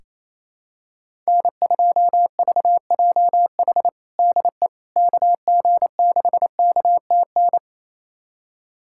Morse code, heard as a keyed tone, about 28 wpm, 700 Hz. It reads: N2VJH DE KG6KTN
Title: USA Callsign DE Callsign 28wpm